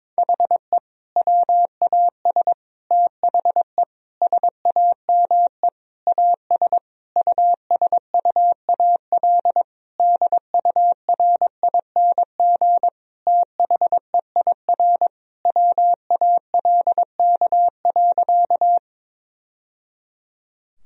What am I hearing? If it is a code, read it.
Morse code, heard as a keyed tone, about 22 wpm, 700 Hz. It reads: HE WAS THE SAME AS USUAL DURING THEIR WALK.
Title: Sentences from Top 1000 Words - Review Part 2 - 22wpm